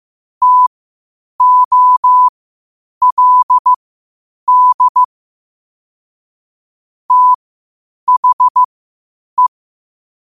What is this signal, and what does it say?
Morse code, heard as a keyed tone, about 15 wpm, spaced as for 8 wpm, 1000 Hz. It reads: TOLD THE
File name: qotdfiles / 8wpm